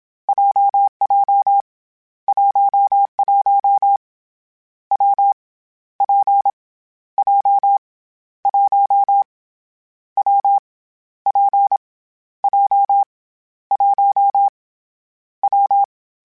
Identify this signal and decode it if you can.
Morse code, heard as a keyed tone, about 27 wpm, 800 Hz. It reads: JJ 11 W P J 1 W P J 1 W
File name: Variable Speed / Variable Speed-mp3s / WPJ WPJ1 sp @25